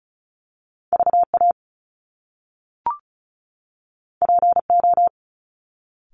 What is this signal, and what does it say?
Morse code, heard as a keyed tone, about 36 wpm, 700 Hz. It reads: VA PO
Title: Top Two Letter Combinations in Words 35wpm